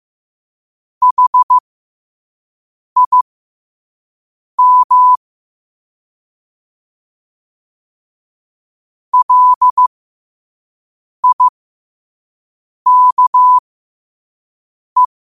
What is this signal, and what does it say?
Morse code, heard as a keyed tone, about 15 wpm, spaced as for 5 wpm, 1000 Hz. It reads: HIM LIKE
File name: qotdfiles / 5wpm